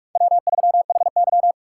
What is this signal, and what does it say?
Morse code, heard as a keyed tone, about 46 wpm, 700 Hz. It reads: W3HY
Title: USA Call Signs - 2x Character Spacing - 45wpm